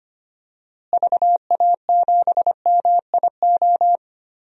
Morse code, heard as a keyed tone, about 25 wpm, 700 Hz. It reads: VA7MIO
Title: DXCC-100 Call Signs - 6 Characters - Repeated 3x - 25wpm